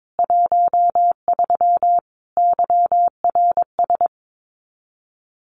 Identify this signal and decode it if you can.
Morse code, heard as a keyed tone, about 22 wpm, 700 Hz. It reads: 13 YRS
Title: Common QSO Phrases 22wpm